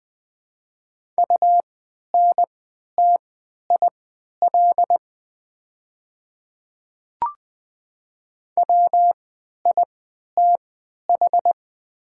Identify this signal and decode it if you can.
Morse code, heard as a keyed tone, about 20 wpm, spaced as for 11 wpm, 690 Hz. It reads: UNTIL WITH
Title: 3x Character Spacing - Top 300 Words - 20wpm